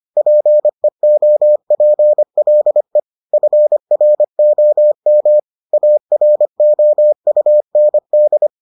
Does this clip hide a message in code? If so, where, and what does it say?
Yes: Morse code from 0.2 to 8.5 s, heard as a keyed tone, about 25 wpm, 600 Hz. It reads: PEOPLE FROM AROUND